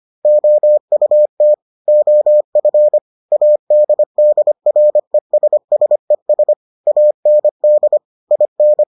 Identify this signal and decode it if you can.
Morse code, heard as a keyed tone, about 25 wpm, 600 Hz. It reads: OUT OF ADDRESSES AND IN